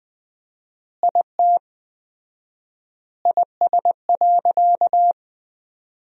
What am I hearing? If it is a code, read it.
Morse code, heard as a keyed tone, about 20 wpm, 700 Hz. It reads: IT IS.
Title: 4x Word-Spacing - Sentences from Top 300 Words - 20wpm